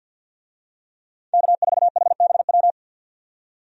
Morse code, heard as a keyed tone, about 52 wpm, 700 Hz. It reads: K4HBW
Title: USA Call Signs - 5 Characters - 50wpm